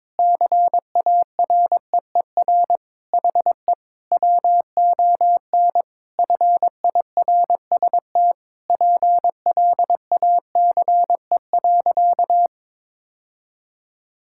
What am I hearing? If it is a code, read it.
Morse code, heard as a keyed tone, about 22 wpm, 700 Hz. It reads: CAREER HE WON FIRST PLACE.
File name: Sentences with 7-Word Phrases - 22wpm